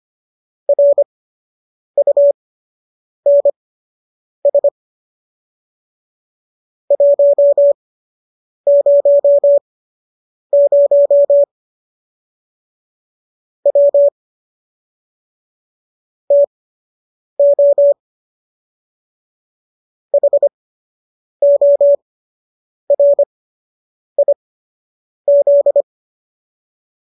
Morse code, heard as a keyed tone, about 25 wpm, spaced as for 8 wpm, 570 Hz. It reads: RUNS 100 W TO HORIZ